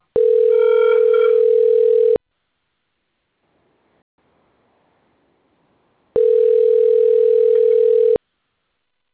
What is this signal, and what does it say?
Ringback tone (North America), 440 + 480 Hz.